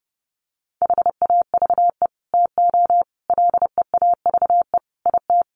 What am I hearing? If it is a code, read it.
Morse code, heard as a keyed tone, about 30 wpm, 700 Hz. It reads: HAVE TO LEAVE IT
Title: Sets of 5 Words Encore2 30wpm